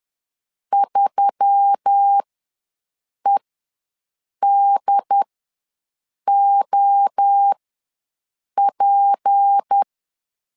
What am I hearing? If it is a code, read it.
Morse code, heard as a keyed tone, about 10 wpm, spaced as for 6 wpm, 800 Hz. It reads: 3EDOP
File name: LEZIONI / 6 PKX0 / 6 P ws 10wpm 3751